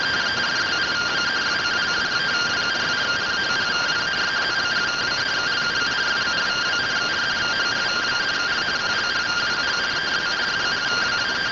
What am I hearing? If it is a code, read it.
Two-tone teleprinter signal, about 170 Hz shift.